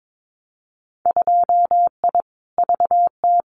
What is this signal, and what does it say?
Morse code, heard as a keyed tone, about 22 wpm, 700 Hz. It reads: 2I VT
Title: Winter Field Day Exchanges 22wpm